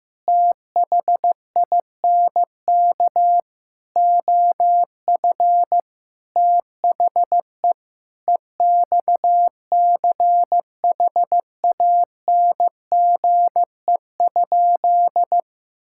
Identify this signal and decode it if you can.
Morse code, heard as a keyed tone, about 15 wpm, 700 Hz. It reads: THINK OF THE EXCHANGE?
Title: Sentences from Top 1000 Words - Review Part 6 - 15wpm